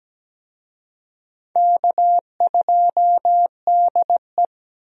Morse code, heard as a keyed tone, about 17 wpm, 700 Hz. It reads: K2DE
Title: USA Call Signs - 4 Characters - Repeated 2x - 17wpm